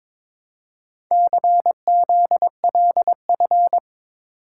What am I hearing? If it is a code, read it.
Morse code, heard as a keyed tone, about 22 wpm, 700 Hz. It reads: CZLF
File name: practice / 22wpm / Four Letter-Number Combinations - 22wpm